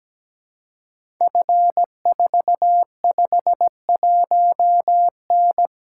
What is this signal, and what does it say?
Morse code, heard as a keyed tone, about 17 wpm, 700 Hz. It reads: F451N
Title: Code Groups - 5 Letter:Numbers - 17wpm